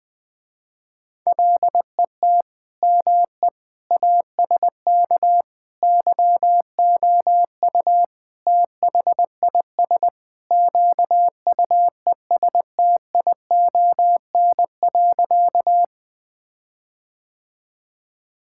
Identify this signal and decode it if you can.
Morse code, heard as a keyed tone, about 20 wpm, 700 Hz. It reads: LET ME ASK YOU THIS QUESTION.
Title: Sentences with 5-Word Phrases - Spoken Beginning - 20wpm